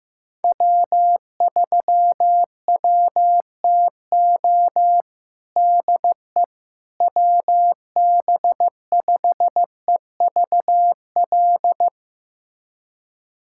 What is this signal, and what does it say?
Morse code, heard as a keyed tone, about 16 wpm, 700 Hz. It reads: W3WTO DE WB5EVL
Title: USA Callsign DE Callsign 15wpm